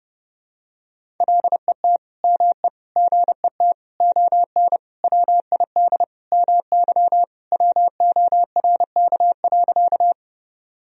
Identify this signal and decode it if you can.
Morse code, heard as a keyed tone, about 29 wpm, 700 Hz. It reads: LET ME GET ON WID MY WORK.
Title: Sentences from Top 300 Words and Abbreviations 30wpm